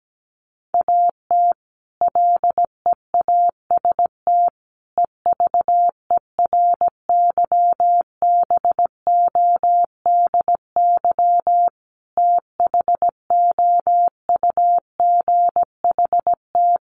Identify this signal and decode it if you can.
Morse code, heard as a keyed tone, about 17 wpm, 700 Hz. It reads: AT LEAST EVERYBODY THOUGHT
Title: Sentences from Top 1400 Words 17wpm